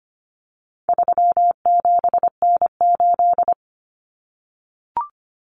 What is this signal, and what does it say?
Morse code, heard as a keyed tone, about 25 wpm, 700 Hz. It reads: 37N8
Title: Common Cut Numbers - 4-Digit 25wpm